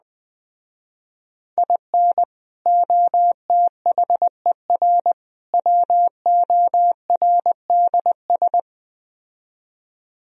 Morse code, heard as a keyed tone, about 21 wpm, 700 Hz. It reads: IN OTHER WORDS